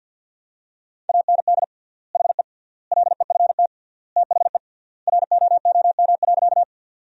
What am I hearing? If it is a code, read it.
Morse code, heard as a keyed tone, about 53 wpm, 700 Hz. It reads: AND HE LEFT THE ROOM.
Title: 3x Word-Spacing - Sentences from Top 300 Words - 50wpm